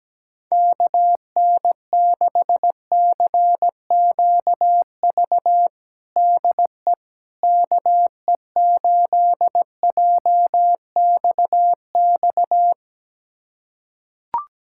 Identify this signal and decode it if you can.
Morse code, heard as a keyed tone, about 17 wpm, 700 Hz. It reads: KN6CQV DE KE8JXX
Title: USA Callsign DE Callsign - 6 Characters - 17wpm